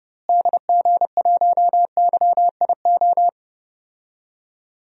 Morse code, heard as a keyed tone, about 30 wpm, 700 Hz. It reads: DG1YIO
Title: DXCC-100 Call Signs - 6 Characters - Repeated 2x - 30wpm